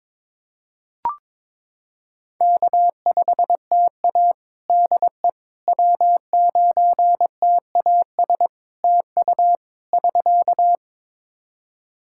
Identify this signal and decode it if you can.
Morse code, heard as a keyed tone, about 22 wpm, 700 Hz. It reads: K5TA DE W9TAS TU <SK>
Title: Practice QSO - Exchange 4 - Ending - 22wpm